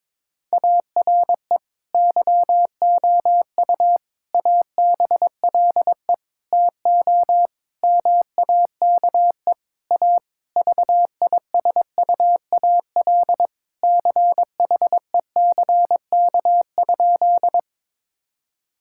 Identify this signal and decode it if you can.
Morse code, heard as a keyed tone, about 22 wpm, 700 Hz. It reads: ARE YOU ABLE TO MAKE A VISUAL CHECK?
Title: Sentences from Top 1000 Words - Review Part 6 - 22wpm